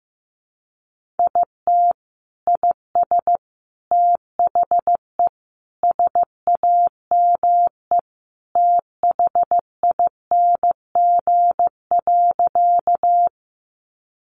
Morse code, heard as a keyed tone, about 15 wpm, 700 Hz. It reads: IT IS THE SAME THING.
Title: Sentences from Top 200 Words 15wpm